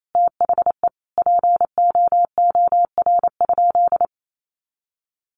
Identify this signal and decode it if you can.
Morse code, heard as a keyed tone, about 28 wpm, 700 Hz. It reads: THE POOR?
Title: Sentences from Top 1100 Words 28wpm